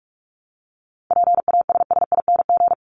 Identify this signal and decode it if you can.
Morse code, heard as a keyed tone, about 44 wpm, 700 Hz. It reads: PASSING
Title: ING Suffix - Top 100 Words with Base Words 45wpm